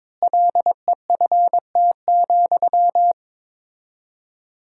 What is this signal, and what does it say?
Morse code, heard as a keyed tone, about 22 wpm, 700 Hz. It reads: LEFT,
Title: 4x Word-Spacing - Sentences from Top 400 Words - 22wpm